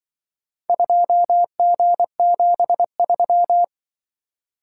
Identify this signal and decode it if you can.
Morse code, heard as a keyed tone, about 24 wpm, 700 Hz. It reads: 2G73
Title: Speed-Racing - Four Letter-Number Combinations - 20wpm